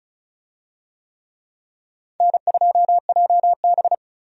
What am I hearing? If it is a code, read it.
Morse code, heard as a keyed tone, about 36 wpm, 700 Hz. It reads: N2JB
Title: USA Call Signs - 4 Characters - 35wpm